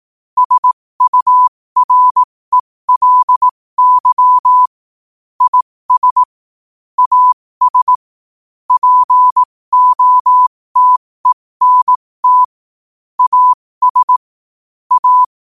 Morse code, heard as a keyed tone, about 18 wpm, spaced as for 15 wpm, 1000 Hz. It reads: SURELY IS AS POTENT AS A
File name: qotdfiles / 15wpm